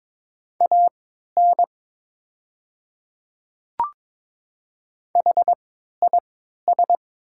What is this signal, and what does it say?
Morse code, heard as a keyed tone, about 22 wpm, spaced as for 12 wpm, 690 Hz. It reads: AN HIS
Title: 3x Character Spacing - Top 200 Words - 22wpm